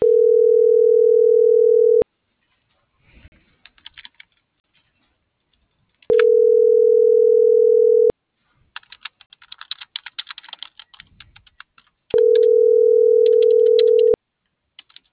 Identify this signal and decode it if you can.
Ringback tone (North America), 440 + 480 Hz.